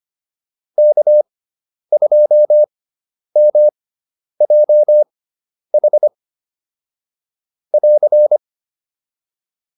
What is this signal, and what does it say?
Morse code, heard as a keyed tone, about 25 wpm, spaced as for 10 wpm, 600 Hz. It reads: K2MJH <AR>
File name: QSO402 10